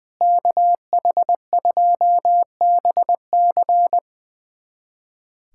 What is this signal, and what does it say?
Morse code, heard as a keyed tone, about 21 wpm, 700 Hz. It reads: KH2BC
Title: USA Callsign DE Callsign - 5 Characters - 20wpm